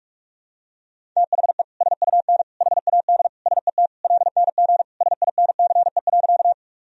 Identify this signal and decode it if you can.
Morse code, heard as a keyed tone, about 44 wpm, 700 Hz. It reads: THE SUN HAD SET LNG SINCE.
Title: Sentences from Top 300 Words and Abbreviations 45wpm